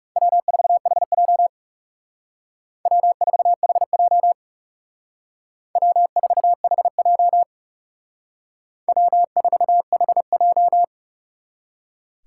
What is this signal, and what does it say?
Morse code, heard as a keyed tone, about 38 wpm, 700 Hz. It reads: W4HJ W4HJ W4HJ W4HJ